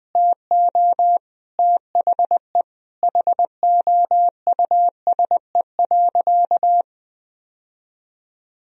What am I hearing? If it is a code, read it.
Morse code, heard as a keyed tone, about 20 wpm, 700 Hz. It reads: TO THE HOUSE.
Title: Sentences from Top 1000 Words - Review Part 4 - 20wpm